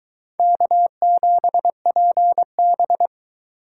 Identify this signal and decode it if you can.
Morse code, heard as a keyed tone, about 23 wpm, 700 Hz. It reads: K7PB